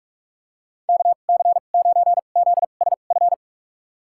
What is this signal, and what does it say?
Morse code, heard as a keyed tone, about 46 wpm, spaced as for 33 wpm, 700 Hz. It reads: KC9ZSF
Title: USA Call Signs - 6 Characters - 2x Character Spacing - 45wpm